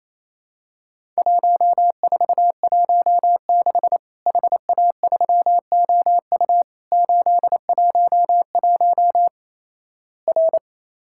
Morse code, heard as a keyed tone, about 28 wpm, 700 Hz. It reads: 1416 HA3OU 811 R
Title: Worked All Europe DX Contest - QTC Exchange - 28wpm